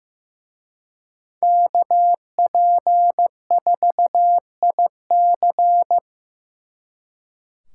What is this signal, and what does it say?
Morse code, heard as a keyed tone, about 18 wpm, 700 Hz. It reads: KP4IC
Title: Speed Racing - USA 2x2 Call Signs 15wpm